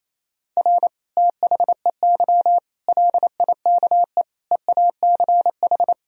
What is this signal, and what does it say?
Morse code, heard as a keyed tone, about 28 wpm, 700 Hz. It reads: R THEY LIKE EACH
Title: Sentences from Top 200 Words and Abbreviations 28wpm